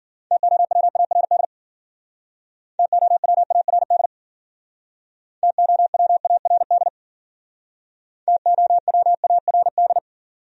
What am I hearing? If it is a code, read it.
Morse code, heard as a keyed tone, about 52 wpm, 700 Hz. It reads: TOWARD TOWARD TOWARD TOWARD